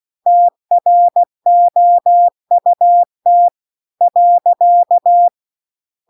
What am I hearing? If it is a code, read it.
Morse code, heard as a keyed tone, about 16 wpm, 700 Hz. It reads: TROUT .
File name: CW-The-Big-Twoheaded-River-2-16WPM